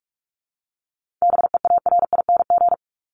Morse code, heard as a keyed tone, about 44 wpm, 700 Hz. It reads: BEARING
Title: ING Suffix - Top 100 Words with Base Words 45wpm